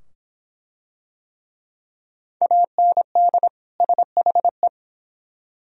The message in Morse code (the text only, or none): AND SHE